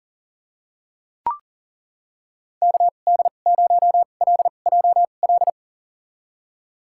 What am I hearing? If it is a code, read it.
Morse code, heard as a keyed tone, about 40 wpm, spaced as for 29 wpm, 700 Hz. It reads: KD0LJL
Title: USA Call Signs - 2x Character Spacing - 40wpm